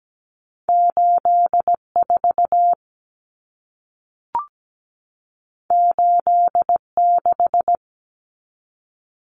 Morse code, heard as a keyed tone, about 17 wpm, 700 Hz. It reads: 84 86
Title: Numbers - 2-Digit 17wpm